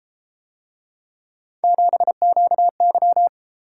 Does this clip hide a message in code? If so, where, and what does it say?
Yes: Morse code from 1.6 to 3.3 s, heard as a keyed tone, about 30 wpm, 700 Hz. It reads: 7QY